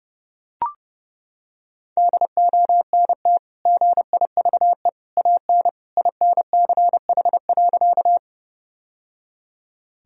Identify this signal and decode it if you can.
Morse code, heard as a keyed tone, about 30 wpm, 700 Hz. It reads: DONT GIVE AN INCH.
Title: Sentences from Top 1000 Words - Review Part 3 - 30wpm